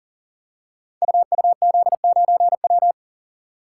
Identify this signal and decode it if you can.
Morse code, heard as a keyed tone, about 39 wpm, 700 Hz. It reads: UUZ9W
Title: Code Groups - 5 Letter:Numbers - 40wpm